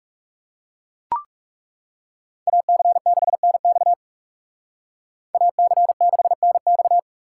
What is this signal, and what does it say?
Morse code, heard as a keyed tone, about 41 wpm, 700 Hz. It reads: AC6NX AC6NX